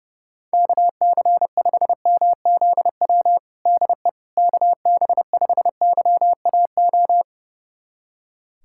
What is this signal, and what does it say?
Morse code, heard as a keyed tone, about 32 wpm, 700 Hz. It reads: KC5MZW DE KB5YAO
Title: USA Callsign DE Callsign - 6 Characters - 30wpm